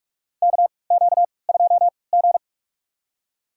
Morse code, heard as a keyed tone, about 46 wpm, spaced as for 26 wpm, 700 Hz. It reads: KQ2G
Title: USA Call Signs - 3x Character Spacing - 45wpm